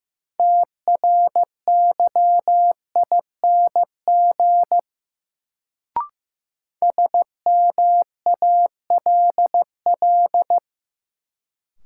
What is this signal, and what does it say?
Morse code, heard as a keyed tone, about 15 wpm, 700 Hz. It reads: TRYING SMALL
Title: Sets of 2 Words - New Word Drill - 15wpm